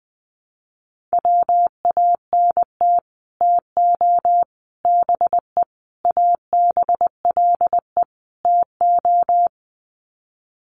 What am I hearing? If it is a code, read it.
Morse code, heard as a keyed tone, about 20 wpm, 700 Hz. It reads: WANT TO BE ABLE TO
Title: Sets of 5 Words 20wpm